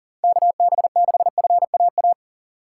Morse code, heard as a keyed tone, about 38 wpm, 700 Hz. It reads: KB6FAA
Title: USA Call Signs - 6 Characters - 2x Character Spacing - 40wpm